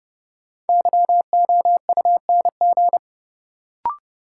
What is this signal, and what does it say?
Morse code, heard as a keyed tone, about 30 wpm, 700 Hz. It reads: YOUNG
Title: Sentences from Top 400 Words - New Word Drill - 30wpm